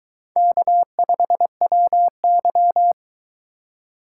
Morse code, heard as a keyed tone, about 22 wpm, 700 Hz. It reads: K5WY